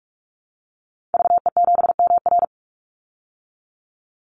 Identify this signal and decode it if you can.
Morse code, heard as a keyed tone, about 44 wpm, 700 Hz. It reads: VE7MR
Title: Call Signs Repeated 3x - 45wpm